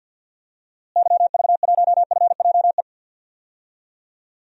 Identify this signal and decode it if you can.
Morse code, heard as a keyed tone, about 50 wpm, 700 Hz. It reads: YV1FJE
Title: DXCC-100 Call Signs - 6 Characters - Repeated 3x - 50wpm